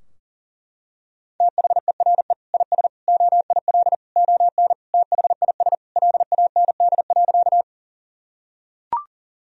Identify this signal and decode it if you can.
Morse code, heard as a keyed tone, about 40 wpm, 700 Hz. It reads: THERE IS OIL ON THIS LAND.
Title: Sentences from Top 1000 Words - Review Part 4 - 40wpm